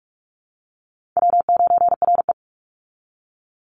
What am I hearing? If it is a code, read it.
Morse code, heard as a keyed tone, about 44 wpm, 700 Hz. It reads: W9RE
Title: Call Signs Repeated 2x - 45wpm